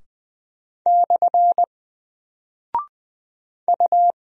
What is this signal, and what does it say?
Morse code, heard as a keyed tone, about 21 wpm, 700 Hz. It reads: / U
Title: Single Character (Letter:Number:Punctuation:Prosign) Alt 20wpm